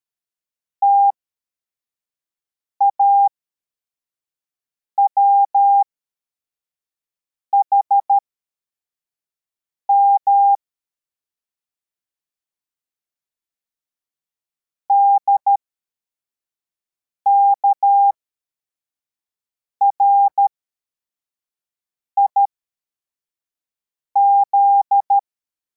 Morse code, heard as a keyed tone, about 13 wpm, spaced as for 4 wpm, 800 Hz. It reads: TAWHM DKRIZ
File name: formation / cw / Text84